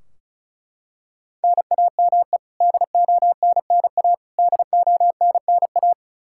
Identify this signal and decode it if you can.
Morse code, heard as a keyed tone, about 34 wpm, 700 Hz. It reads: NAME DONNA DONNA
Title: K1USN SST QSO Practice 35wpm